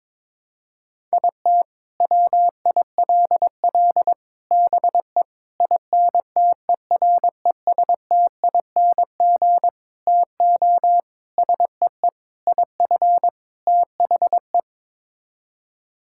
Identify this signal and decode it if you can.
Morse code, heard as a keyed tone, about 22 wpm, 700 Hz. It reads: IT WILL BE INTERESTING TO SEE IF THE
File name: practice / 22wpm / Sets of 8 Words - 22wpm